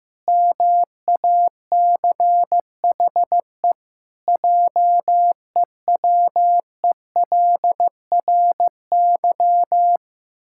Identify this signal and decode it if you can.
Morse code, heard as a keyed tone, about 15 wpm, 700 Hz. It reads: MACHE JEWELRY
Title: Pangram Challenge v2 - 15wpm